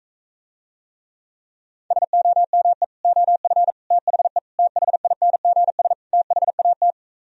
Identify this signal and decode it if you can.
Morse code, heard as a keyed tone, about 45 wpm, 700 Hz. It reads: SOME OF THE THINGS THAT